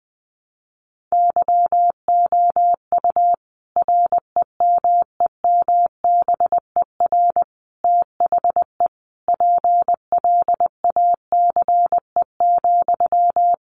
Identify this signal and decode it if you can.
Morse code, heard as a keyed tone, about 20 wpm, 700 Hz. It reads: YOU REMEMBER THE PLACE,
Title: Sentences from Top 400 Words 20wpm